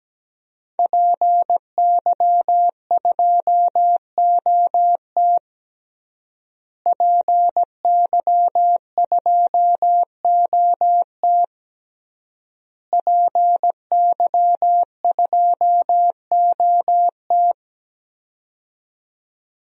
Morse code, heard as a keyed tone, about 17 wpm, 700 Hz. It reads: PY2OT PY2OT PY2OT
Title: DXCC-100 Call Signs - 5 Characters - Repeated 3x - 17wpm